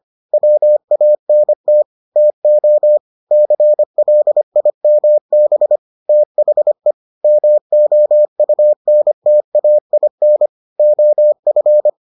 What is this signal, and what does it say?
Morse code, heard as a keyed tone, about 25 wpm, 600 Hz. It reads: WANT TO CLIMB THE MOUNTAIN OF